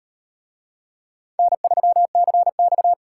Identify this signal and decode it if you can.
Morse code, heard as a keyed tone, about 38 wpm, 700 Hz. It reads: N3CX